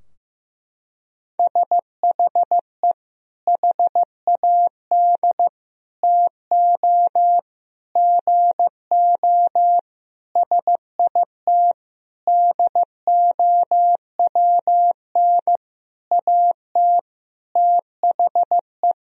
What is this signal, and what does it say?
Morse code, heard as a keyed tone, about 15 wpm, 700 Hz. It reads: SHE HAD TO GO SIT DOWN AT THE
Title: Sentences from Top 1000 Words - Review Part 2 - 15wpm